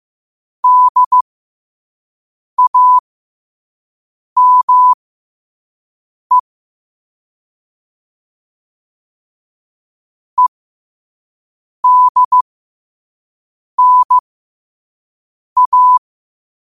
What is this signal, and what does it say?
Morse code, heard as a keyed tone, about 15 wpm, spaced as for 5 wpm, 1000 Hz. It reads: DAME EDNA
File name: qotdfiles / 5wpm